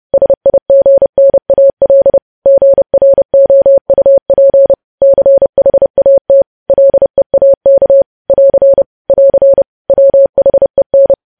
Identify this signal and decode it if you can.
Morse code, heard as a keyed tone, about 30 wpm, 550 Hz. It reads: SIGNAL GROUP CHAT LEAK <AR> <AR> WHEN
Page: news headlines in morse code at 30 wpm